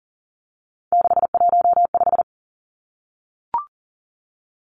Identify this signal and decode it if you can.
Morse code, heard as a keyed tone, about 40 wpm, 700 Hz. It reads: 615
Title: Numbers - 3-Digit 40wpm